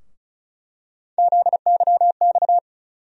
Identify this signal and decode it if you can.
Morse code, heard as a keyed tone, about 36 wpm, 700 Hz. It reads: ZYX
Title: USA Call Sign Postfixes - 3 Letters 35wpm